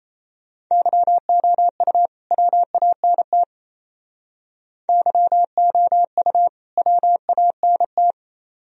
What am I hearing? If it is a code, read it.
Morse code, heard as a keyed tone, about 33 wpm, 700 Hz. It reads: YOU WANT YOU WANT
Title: Speed-Racing - Sets of 2 Words 28wpm